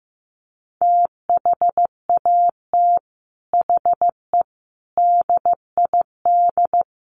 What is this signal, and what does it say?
Morse code, heard as a keyed tone, about 15 wpm, 700 Hz. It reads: THAT HE DID
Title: Sets of 4 Words 15wpm